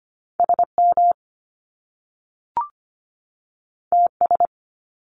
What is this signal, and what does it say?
Morse code, heard as a keyed tone, about 25 wpm, 700 Hz. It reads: SM TS
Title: Top Two Letter Combinations in Words 25wpm